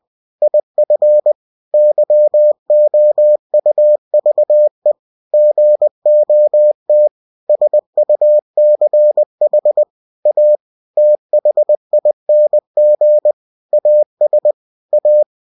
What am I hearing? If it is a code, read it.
Morse code, heard as a keyed tone, about 20 wpm, 600 Hz. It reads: IF YOUVE GOT SUCH A THING AS A
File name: cw-books / winnie-the-pooh / winnie-20wpm-page-0004